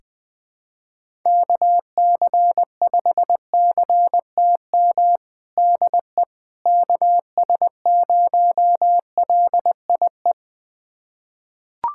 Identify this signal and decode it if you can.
Morse code, heard as a keyed tone, about 20 wpm, 700 Hz. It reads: KC5CTM DE KS0LIE
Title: USA Callsign DE Callsign - 6 Characters - 20wpm